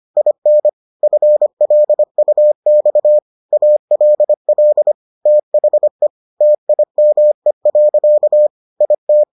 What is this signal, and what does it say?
Morse code, heard as a keyed tone, about 25 wpm, 600 Hz. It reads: IN FLUX ALL THE TIME. IT